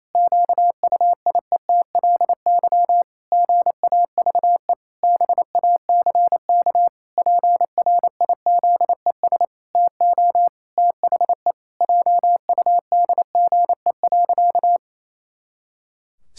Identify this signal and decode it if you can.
Morse code, heard as a keyed tone, about 28 wpm, 700 Hz. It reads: QUIETLY GAVE BACK PRIZES TO THE JUDGE.
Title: Pangram Challenge v2 - 28wpm